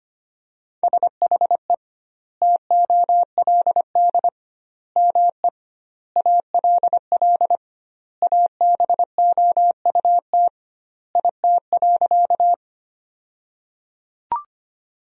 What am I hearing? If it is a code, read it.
Morse code, heard as a keyed tone, about 25 wpm, 700 Hz. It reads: SHE TOLD ME ALL ABOUT IT.
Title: 2x Word-Spacing - Sentences from Top 400 Words - 25wpm